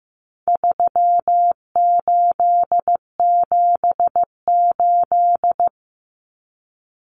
Morse code, heard as a keyed tone, about 15 wpm, 700 Hz. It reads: 3878
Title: Numbers - 4-Digit 15wpm